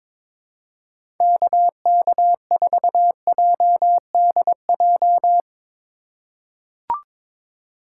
Morse code, heard as a keyed tone, about 22 wpm, 700 Hz. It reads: KK4JDJ
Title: USA Call Signs - 4x Character Spacing - 22wpm